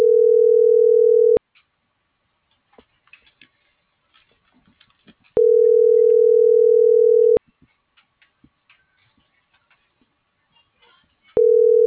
Ringback tone (North America), 440 + 480 Hz.